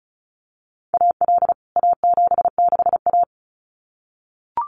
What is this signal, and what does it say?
Morse code, heard as a keyed tone, about 34 wpm, 700 Hz. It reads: AL A76A
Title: CWT Exchange 35wpm